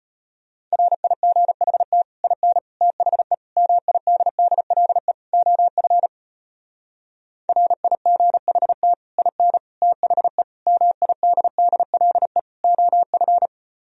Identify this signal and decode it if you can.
Morse code, heard as a keyed tone, about 33 wpm, 700 Hz. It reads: RIGHT IN THE MIDDLE OF RIGHT IN THE MIDDLE OF